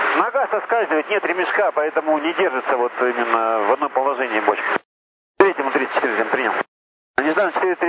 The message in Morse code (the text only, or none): none